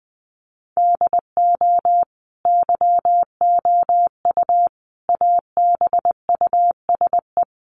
Morse code, heard as a keyed tone, about 19 wpm, 700 Hz. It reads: DO YOU ABUSE
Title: Sentences from Top 1300 Words 20wpm